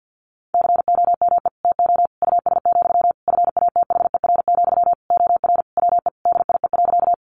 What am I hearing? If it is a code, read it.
Morse code, heard as a keyed tone, about 50 wpm, 700 Hz. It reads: COME TO US, FATHER, OR WE DIE.
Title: Sentences from Top 1300 Words 50wpm